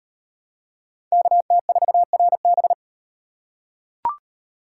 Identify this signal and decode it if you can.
Morse code, heard as a keyed tone, about 38 wpm, 700 Hz. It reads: KT4RB